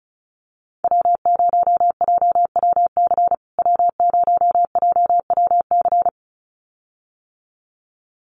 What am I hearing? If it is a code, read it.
Morse code, heard as a keyed tone, about 34 wpm, 700 Hz. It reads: W0JWC W0JWC
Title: USA Call Signs Repeated 2x 35wpm